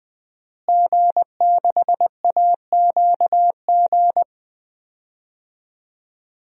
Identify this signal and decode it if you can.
Morse code, heard as a keyed tone, about 20 wpm, 700 Hz. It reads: G6AQG
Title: DXCC-100 Call Signs - 5 Characters - Repeated 3x - 20wpm